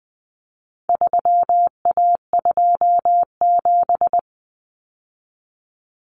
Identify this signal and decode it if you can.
Morse code, heard as a keyed tone, about 20 wpm, 700 Hz. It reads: 3A27
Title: Common Cut Numbers - 4-Digit 20wpm